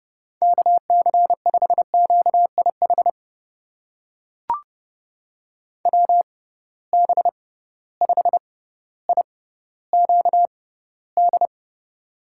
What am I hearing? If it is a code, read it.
Morse code, heard as a keyed tone, about 30 wpm, spaced as for 15 wpm, 700 Hz. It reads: KC5QIH WB5IQD
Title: USA Call Signs - 6 Characters - 6x Character Spacing - 30wpm